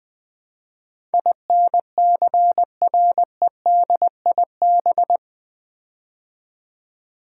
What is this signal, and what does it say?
Morse code, heard as a keyed tone, about 21 wpm, 700 Hz. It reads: INCREDIB
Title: Build A Word - 4 Letters At A Time - 10-Letter Words - 20wpm